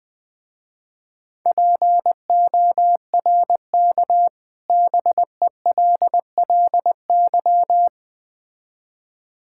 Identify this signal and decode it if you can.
Morse code, heard as a keyed tone, about 20 wpm, 700 Hz. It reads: PORK BELLY
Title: Difficult Words with Hints - 20wpm